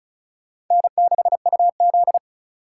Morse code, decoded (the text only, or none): N6UZ